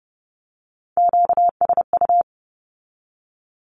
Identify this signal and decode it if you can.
Morse code, heard as a keyed tone, about 30 wpm, 700 Hz. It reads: QSU
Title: Q-Codes with Spoken Meaning - 30wpm